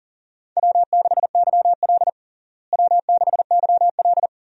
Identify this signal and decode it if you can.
Morse code, heard as a keyed tone, about 40 wpm, 700 Hz. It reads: W6YL W6YL
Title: USA Call Signs - 4 Characters - Repeated 3x - 40wpm